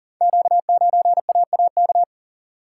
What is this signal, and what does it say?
Morse code, heard as a keyed tone, about 38 wpm, 690 Hz. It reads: Q9AAK
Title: Code Groups - 5 Letter:Numbers - 40wpm